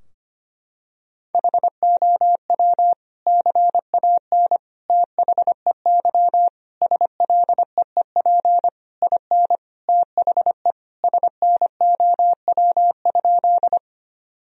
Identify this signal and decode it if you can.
Morse code, heard as a keyed tone, about 25 wpm, 700 Hz. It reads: HOW CAN THEY SLEEP IN THE SNOW?
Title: Sentences from Top 1000 Words - Review Part 3 - 25wpm